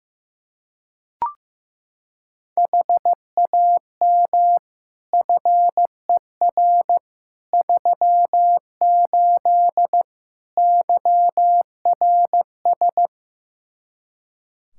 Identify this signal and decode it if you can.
Morse code, heard as a keyed tone, about 15 wpm, 700 Hz. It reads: HAM FER 38 YRS
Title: K1USN SST QSO Practice 15wpm